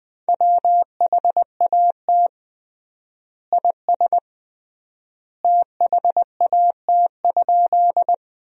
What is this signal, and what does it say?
Morse code, heard as a keyed tone, about 21 wpm, 700 Hz. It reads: WHAT IS THAT?
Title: 3x Word-Spacing - Sentences from Top 100 Words - 20wpm